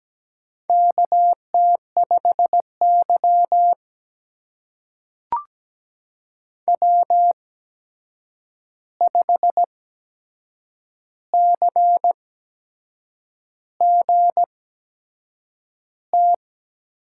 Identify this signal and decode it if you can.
Morse code, heard as a keyed tone, about 17 wpm, spaced as for 7 wpm, 700 Hz. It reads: KT5Y W5CGT
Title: USA Call Signs - 8x Character Spacing - 17wpm